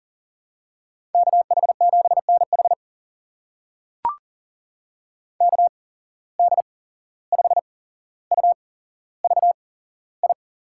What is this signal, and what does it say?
Morse code, heard as a keyed tone, about 40 wpm, spaced as for 16 wpm, 700 Hz. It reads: KH7NH KD5UVI
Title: USA Call Signs - 8x Character Spacing - 40wpm